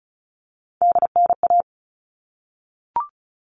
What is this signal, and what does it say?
Morse code, heard as a keyed tone, about 34 wpm, 700 Hz. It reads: DNA
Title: All Cut Numbers - 3-Digit 35wpm